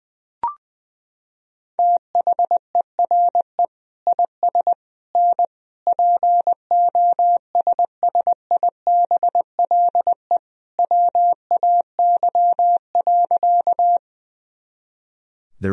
Morse code, heard as a keyed tone, about 20 wpm, 700 Hz. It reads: THERE IS N POSSIBLE WAY.
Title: Sentences from Top 500 Words and Abbreviations 20wpm